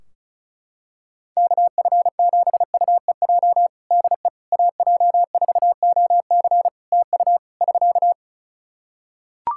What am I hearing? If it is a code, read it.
Morse code, heard as a keyed tone, about 36 wpm, 700 Hz. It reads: KF7UEJ DE AJ4OC TU <SK>
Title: Practice QSO - Exchange 4 - Ending - 35wpm